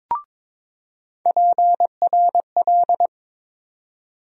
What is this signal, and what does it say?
Morse code, heard as a keyed tone, about 22 wpm, 700 Hz. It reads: PRL
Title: USA Call Sign Postfixes - 3 Letters 22wpm